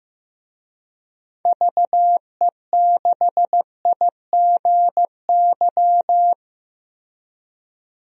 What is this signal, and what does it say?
Morse code, heard as a keyed tone, about 15 wpm, 700 Hz. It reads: VE6IGY
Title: DXCC-100 Call Signs - 6 Characters - Repeated 2x - 15wpm